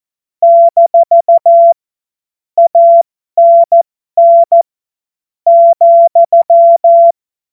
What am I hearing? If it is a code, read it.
Morse code, heard as a keyed tone, about 14 wpm, spaced as for 10 wpm, 680 Hz. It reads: - ANN,
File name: Morsekurs / Witze